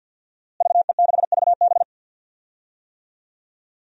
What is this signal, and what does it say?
Morse code, heard as a keyed tone, about 52 wpm, 700 Hz. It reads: VE6VB
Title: DXCC-100 Call Signs - 5 Characters - Repeated 3x - 50wpm